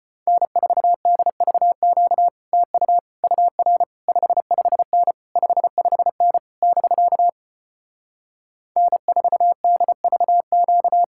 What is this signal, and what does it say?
Morse code, heard as a keyed tone, about 33 wpm, 700 Hz. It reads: N4DVQ TU UR 55N 55N <BK> N4DVQ